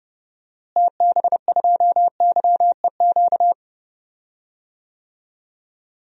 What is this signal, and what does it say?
Morse code, heard as a keyed tone, about 30 wpm, 700 Hz. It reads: TB2YEQ
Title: DXCC-100 Call Signs - 6 Characters - Repeated 2x - 30wpm